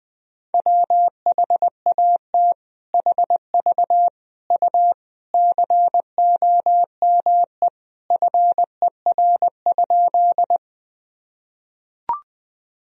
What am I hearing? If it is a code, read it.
Morse code, heard as a keyed tone, about 20 wpm, 700 Hz. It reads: WHAT HV U COME FER?
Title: Sentences from Top 100 Words and Abbreviations 20wpm